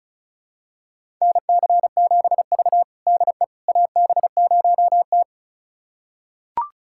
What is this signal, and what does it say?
Morse code, heard as a keyed tone, about 36 wpm, 700 Hz. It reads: NC7V DE AB0T
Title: USA Callsign DE Callsign - 4 Characters - 35wpm